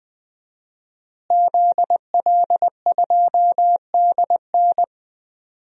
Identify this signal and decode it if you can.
Morse code, heard as a keyed tone, about 21 wpm, 700 Hz. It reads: ZL2DN
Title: DXCC-100 Call Signs - 5 Characters - Repeated 2x - 20wpm